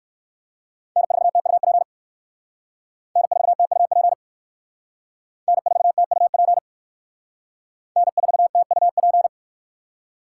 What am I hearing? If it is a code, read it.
Morse code, heard as a keyed tone, about 57 wpm, 700 Hz. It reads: N4TUP N4TUP N4TUP N4TUP